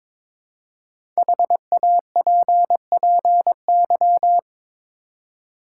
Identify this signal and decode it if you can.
Morse code, heard as a keyed tone, about 22 wpm, 700 Hz. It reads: HAPPY
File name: 5 Letter Words - First 3 Letters Given - 22wpm